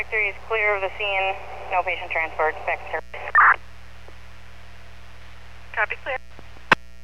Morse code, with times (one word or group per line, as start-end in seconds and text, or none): none